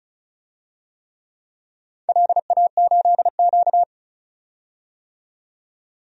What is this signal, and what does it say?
Morse code, heard as a keyed tone, about 36 wpm, 700 Hz. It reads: LA8Q
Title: DXCC-100 Call Signs - 4 Characters - 35wpm